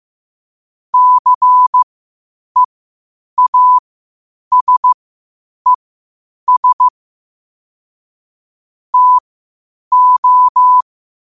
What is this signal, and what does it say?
Morse code, heard as a keyed tone, about 15 wpm, spaced as for 8 wpm, 1000 Hz. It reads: CEASES TO
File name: qotdfiles / 8wpm